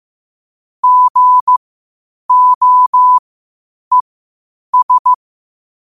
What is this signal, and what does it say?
Morse code, heard as a keyed tone, about 15 wpm, spaced as for 8 wpm, 1000 Hz. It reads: GOES
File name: qotdfiles / 8wpm